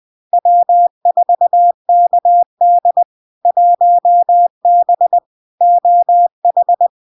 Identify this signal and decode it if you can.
Morse code, heard as a keyed tone, about 20 wpm, 700 Hz. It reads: W4KD 1B OH
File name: LARK-FD-2022-Log-in-CW-20wpm-700-Hz